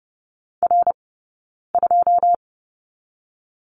Morse code, heard as a keyed tone, about 32 wpm, 700 Hz. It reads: R 2
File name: Lesson 20d - Number 3 - Two Characters 30wpm